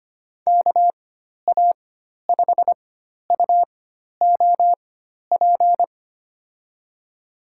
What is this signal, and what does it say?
Morse code, heard as a keyed tone, about 25 wpm, spaced as for 12 wpm, 700 Hz. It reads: KA5UOP
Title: USA Call Signs - 6 Characters - 4x Character Spacing - 25wpm